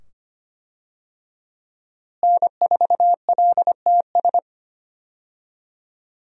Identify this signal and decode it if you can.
Morse code, heard as a keyed tone, about 25 wpm, 700 Hz. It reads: N4LTS
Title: USA Call Signs - 5 Characters - Repeated 2x - 25wpm